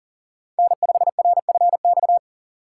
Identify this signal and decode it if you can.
Morse code, heard as a keyed tone, about 43 wpm, 690 Hz. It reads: N5RFX
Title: USA Call Signs - 5 Characters - 6x Character Spacing - 40wpm